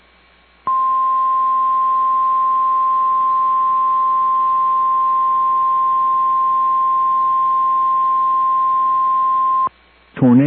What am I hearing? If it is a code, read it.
Emergency Alert System (SAME) tones (the 1050 Hz attention tone). It is heard from 0.6 to 9.7 s.